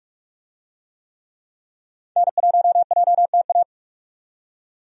Morse code, heard as a keyed tone, about 47 wpm, 700 Hz. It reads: N1JTA
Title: USA Call Signs - 5 Characters - 45wpm